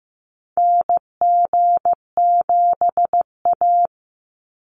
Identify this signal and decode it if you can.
Morse code, heard as a keyed tone, about 15 wpm, 700 Hz. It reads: NG7A
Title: NA SOTA Activator Callsigns 15wpm